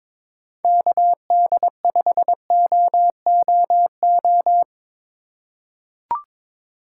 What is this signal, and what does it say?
Morse code, heard as a keyed tone, about 22 wpm, 700 Hz. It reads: KD5OOO
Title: USA Call Signs - 6 Characters - 4x Character Spacing - 22wpm